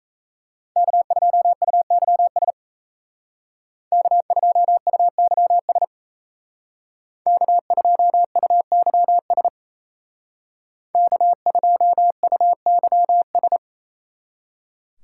Morse code, heard as a keyed tone, about 37 wpm, 700 Hz. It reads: K2UYS K2UYS K2UYS K2UYS